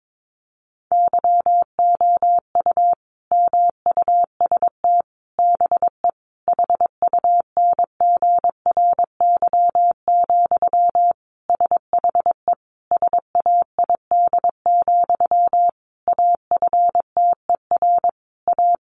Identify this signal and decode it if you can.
Morse code, heard as a keyed tone, about 22 wpm, 700 Hz. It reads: YOU MUST BE HUNGRY, SHE SAID, AFTER A
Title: Sentences from Top 1400 Words 22wpm